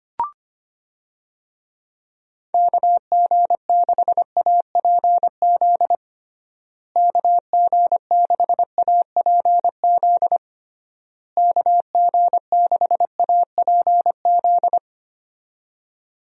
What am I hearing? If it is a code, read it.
Morse code, heard as a keyed tone, about 25 wpm, 700 Hz. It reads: KG6APZ KG6APZ KG6APZ
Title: USA Call Signs - 6 Characters - Repeated 3x - 25wpm